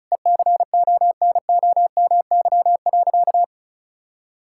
Morse code, heard as a keyed tone, about 34 wpm, 700 Hz. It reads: ECONOMY.
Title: Sentences from Top 1000 Words - Review Part 6 - 35wpm